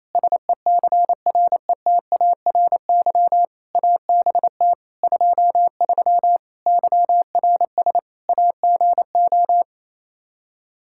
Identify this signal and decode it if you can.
Morse code, heard as a keyed tone, about 28 wpm, 700 Hz. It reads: SECRETARY ABT 23 YRS AGO
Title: K1USN SST QSO Practice 28wpm